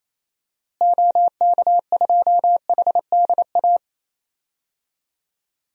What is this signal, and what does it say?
Morse code, heard as a keyed tone, about 28 wpm, 700 Hz. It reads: OK2HDA
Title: DXCC-100 Call Signs - 6 Characters - Repeated 3x - 28wpm